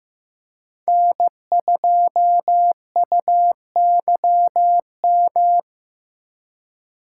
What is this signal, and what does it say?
Morse code, heard as a keyed tone, about 18 wpm, 700 Hz. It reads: N2UYM
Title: Speed Racing - USA 1x3 Call Signs 15wpm